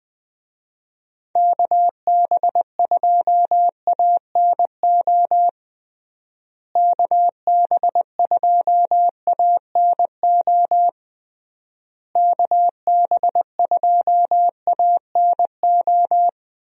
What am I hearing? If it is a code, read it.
Morse code, heard as a keyed tone, about 20 wpm, 700 Hz. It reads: KB2ANO KB2ANO KB2ANO
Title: USA Call Signs - 6 Characters - Repeated 3x - 20wpm